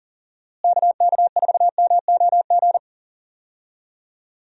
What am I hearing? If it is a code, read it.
Morse code, heard as a keyed tone, about 40 wpm, 700 Hz. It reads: KK4MOG
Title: USA Call Signs - 6 Characters - Repeated 3x - 40wpm